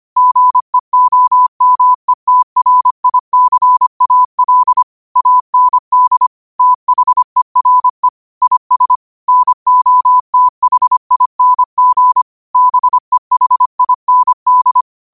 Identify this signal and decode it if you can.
Morse code, heard as a keyed tone, about 25 wpm, 1000 Hz. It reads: GEOMETRICAL AND THERE IS NOTHING BEHIND
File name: qotdfiles / 25wpm